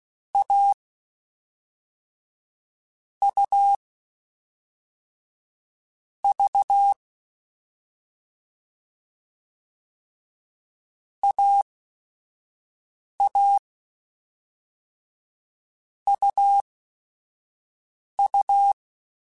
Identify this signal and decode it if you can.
Morse code, heard as a keyed tone, about 16 wpm, spaced as for 5 wpm, 800 Hz. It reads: A U V AA UU